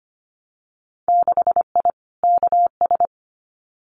Morse code, heard as a keyed tone, about 25 wpm, 700 Hz. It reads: 6I KS
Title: Winter Field Day Exchanges Encore 25wpm